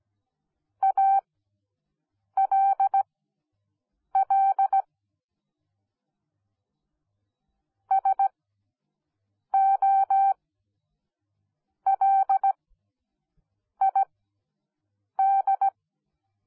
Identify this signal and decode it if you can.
Morse code, heard as a keyed tone, about 17 wpm, spaced as for 6 wpm, 780 Hz. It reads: ALL SOLID